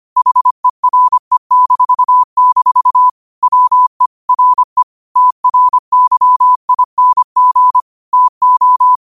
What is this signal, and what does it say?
Morse code, heard as a keyed tone, about 25 wpm, 1000 Hz. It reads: HERE-- WE RE TRYING TO
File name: qotdfiles / 25wpm